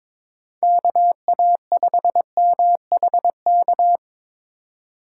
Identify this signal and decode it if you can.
Morse code, heard as a keyed tone, about 23 wpm, 700 Hz. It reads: KA5MHK
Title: USA Call Signs - 2x Character Spacing - 22wpm